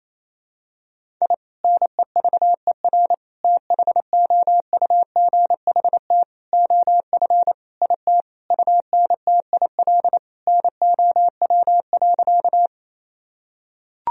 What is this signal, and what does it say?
Morse code, heard as a keyed tone, about 28 wpm, 700 Hz. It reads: I NEVER THOUGHT OF IT UNTIL NOW.
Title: Sentences from Top 1000 Words - Review Part 1 - 28wpm